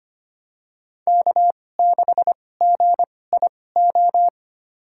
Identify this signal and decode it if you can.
Morse code, heard as a keyed tone, about 25 wpm, spaced as for 18 wpm, 700 Hz. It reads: K6GIO
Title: USA Call Signs - 5 Characters - 2x Character Spacing - 25wpm